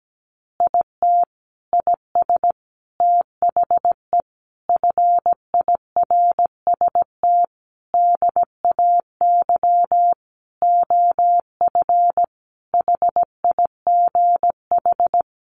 Morse code, heard as a keyed tone, about 17 wpm, 700 Hz. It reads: IT IS THE FIRST DAY OF HIGH
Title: Additional Easy Phrases 17wpm